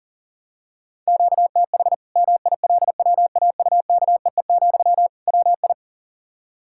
Morse code, heard as a keyed tone, about 37 wpm, 700 Hz. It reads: QTH MILWAUKEE, WI
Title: Practice QSO - Exchange 1 - Beginning - 40wpm